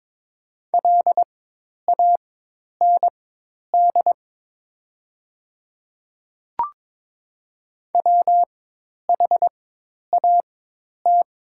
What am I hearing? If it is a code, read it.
Morse code, heard as a keyed tone, about 22 wpm, spaced as for 10 wpm, 700 Hz. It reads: LAND WHAT
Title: 4x Character Spacing - Top 200 Words - 22wpm